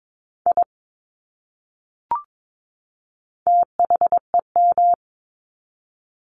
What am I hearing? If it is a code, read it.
Morse code, heard as a keyed tone, about 22 wpm, 700 Hz. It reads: I THEM
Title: Top 200 Words 22wpm